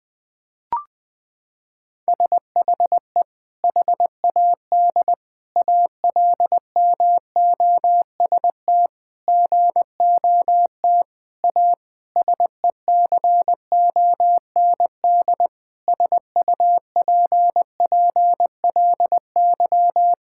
Morse code, heard as a keyed tone, about 20 wpm, 700 Hz. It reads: SHE HAD ALMOST GOT A SECOND SUPPLY
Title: Sentences from Top 1000 Words - Review Part 5 - 20wpm